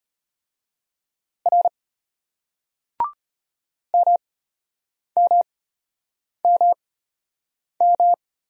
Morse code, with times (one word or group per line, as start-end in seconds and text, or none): 1.5-1.7 R
3.9-8.1 MMMM